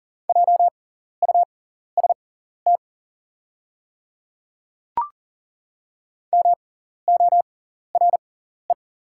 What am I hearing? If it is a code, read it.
Morse code, heard as a keyed tone, about 40 wpm, spaced as for 14 wpm, 690 Hz. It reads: JUST MORE
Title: 6x Character Spacing - Top 200 Words - 40wpm